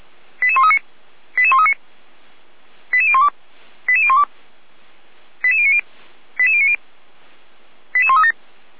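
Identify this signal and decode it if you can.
Five-tone selective call (ZVEI), heard as quick runs of five tones, tones of about 70 ms. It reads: 88139 88139 88012 88012 88090 88090 88137
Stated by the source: ZVEI five-tone call 88190 received via SDR.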